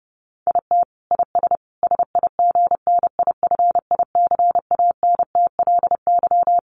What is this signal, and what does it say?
Morse code, heard as a keyed tone, about 30 wpm, 700 Hz. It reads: IT IS SIGNIFICANTLY
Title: Sentences from Top 1500 Words 30wpm